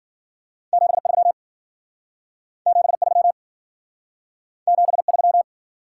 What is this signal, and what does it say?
Morse code, heard as a keyed tone, about 54 wpm, 700 Hz. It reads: 73 73 73